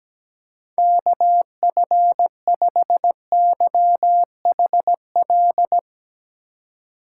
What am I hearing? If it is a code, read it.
Morse code, heard as a keyed tone, about 17 wpm, 700 Hz. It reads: KF5YHL
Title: USA Call Signs - 6 Characters - 2x Character Spacing - 17wpm